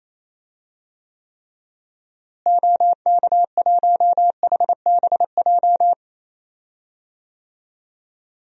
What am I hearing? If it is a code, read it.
Morse code, heard as a keyed tone, about 28 wpm, 700 Hz. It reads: OK1HBJ
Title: DXCC-100 Call Signs - 6 Characters - Repeated 3x - 28wpm